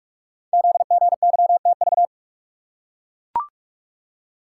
Morse code, heard as a keyed tone, about 44 wpm, 690 Hz. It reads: ZGYTV
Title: Code Groups - 5 Letters - 45wpm